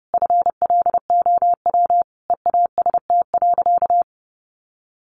Morse code, heard as a keyed tone, about 30 wpm, 700 Hz. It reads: FLOW EAST.
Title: Sentences from Top 1200 Words 30wpm